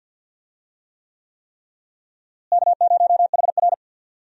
Morse code, heard as a keyed tone, about 51 wpm, 700 Hz. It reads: K0HR
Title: USA Call Signs - 4 Characters - Repeated 3x - 50wpm